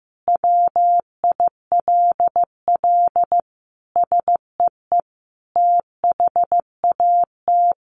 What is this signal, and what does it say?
Morse code, heard as a keyed tone, about 15 wpm, 700 Hz. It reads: WILL SEE THAT
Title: Sentences from Top 800 Words 15wpm